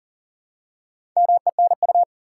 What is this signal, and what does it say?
Morse code, heard as a keyed tone, about 38 wpm, 700 Hz. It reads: MENU
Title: Four Letter Words - 40wpm